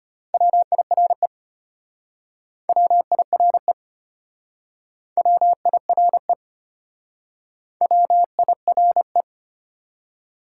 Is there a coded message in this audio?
Yes: Morse code from 0.3 to 9.2 s, heard as a keyed tone, about 33 wpm, 700 Hz. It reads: WIRE WIRE WIRE WIRE